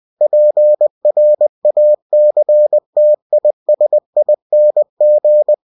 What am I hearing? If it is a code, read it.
Morse code, heard as a keyed tone, about 20 wpm, 600 Hz. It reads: PRACTISING